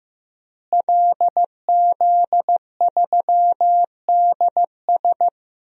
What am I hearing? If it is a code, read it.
Morse code, heard as a keyed tone, about 15 wpm, 700 Hz. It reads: LZ3DS
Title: DXCC-100 Call Signs - 5 Characters - Repeated 2x - 15wpm